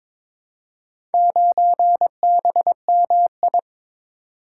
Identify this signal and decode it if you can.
Morse code, heard as a keyed tone, about 22 wpm, 700 Hz. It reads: 9BMI
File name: practice / 22wpm / Four Letter-Number Combinations - 22wpm